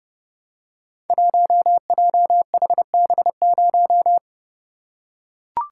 Morse code, heard as a keyed tone, about 30 wpm, 700 Hz. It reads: 1JHB0
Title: Code Groups - 5 Letter:Numbers - 30wpm